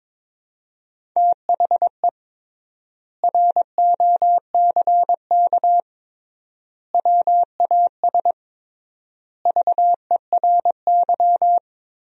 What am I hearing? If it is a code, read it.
Morse code, heard as a keyed tone, about 22 wpm, 700 Hz. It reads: THE ROCK WAS VERY
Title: 3x Word-Spacing - Sentences from Top 400 Words - 22wpm